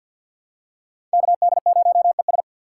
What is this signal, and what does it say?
Morse code, heard as a keyed tone, about 47 wpm, 700 Hz. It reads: KD0ES
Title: USA Call Signs - 4x Character Spacing - 50wpm